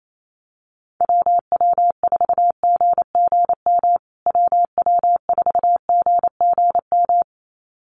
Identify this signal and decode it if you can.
Morse code, heard as a keyed tone, about 28 wpm, 700 Hz. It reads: WW4GGM WW4GGM
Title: USA Call Signs Repeated 2x 28wpm